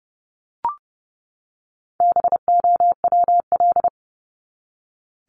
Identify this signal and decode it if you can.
Morse code, heard as a keyed tone, about 30 wpm, 700 Hz. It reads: BOWL
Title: Top 1001-1250 Words 30wpm